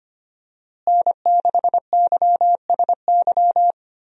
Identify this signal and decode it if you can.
Morse code, heard as a keyed tone, about 25 wpm, 700 Hz. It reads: N6YSY
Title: USA Call Signs - 5 Characters - Repeated 2x - 25wpm